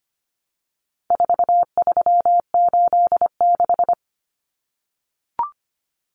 Morse code, heard as a keyed tone, about 25 wpm, 700 Hz. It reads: 4386
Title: Numbers - 4-Digit 25wpm